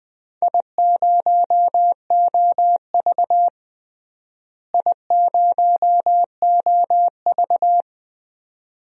Morse code, heard as a keyed tone, about 21 wpm, 700 Hz. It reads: I0OV I0OV
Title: DXCC-100 Call Signs - 4 Characters - Repeated 2x - 20wpm